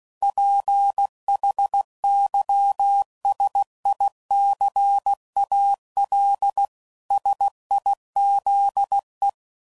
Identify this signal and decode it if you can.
Morse code, heard as a keyed tone, about 16 wpm, 800 Hz. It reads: PHYSICAL SIZE
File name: CW Code Groups @15WPM/Technical Subjects